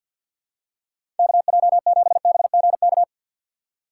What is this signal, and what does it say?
Morse code, heard as a keyed tone, about 53 wpm, 700 Hz. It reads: KJ7BGK
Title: USA Call Signs - 4x Character Spacing - 50wpm